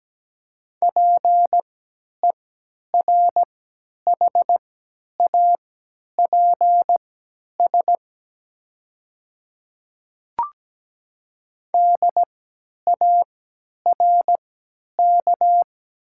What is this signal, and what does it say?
Morse code, heard as a keyed tone, about 17 wpm, spaced as for 10 wpm, 700 Hz. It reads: PERHAPS DARK
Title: 3x Character Spacing - Top 500 Words - 17wpm